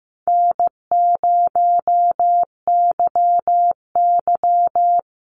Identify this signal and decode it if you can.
Morse code, heard as a keyed tone, about 15 wpm, 700 Hz. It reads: N0YY
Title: Call Signs Repeated 2x - 15wpm